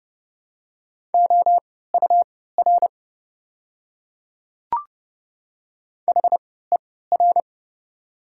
Morse code, heard as a keyed tone, about 30 wpm, spaced as for 17 wpm, 690 Hz. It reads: OUR HER
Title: 3x Character Spacing - Top 500 Words - 30wpm